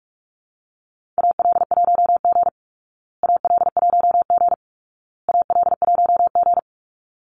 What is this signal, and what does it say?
Morse code, heard as a keyed tone, about 44 wpm, 700 Hz. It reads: AL1G AL1G AL1G
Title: Call Signs Repeated 2x - 45wpm